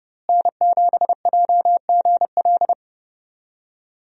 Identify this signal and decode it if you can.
Morse code, heard as a keyed tone, about 28 wpm, 700 Hz. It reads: N7JGL
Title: USA Call Signs - 5 Characters - Repeated 3x - 30wpm